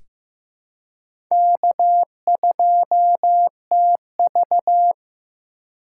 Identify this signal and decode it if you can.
Morse code, heard as a keyed tone, about 15 wpm, 700 Hz. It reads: K2TV
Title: K1USN SST QSO Practice - 15wpm